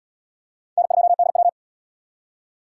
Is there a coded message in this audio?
Yes: Morse code from 0.8 to 1.5 s, heard as a keyed tone, about 72 wpm, 700 Hz. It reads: N2DW